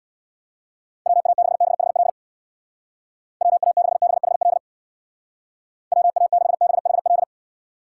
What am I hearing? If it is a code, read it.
Morse code, heard as a keyed tone, about 66 wpm, 700 Hz. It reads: WA6BHL WA6BHL WA6BHL